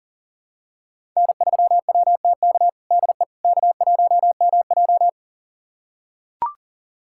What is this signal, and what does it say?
Morse code, heard as a keyed tone, about 40 wpm, 700 Hz. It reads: N3WTK DE K1MJ
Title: USA Callsign DE Callsign 40wpm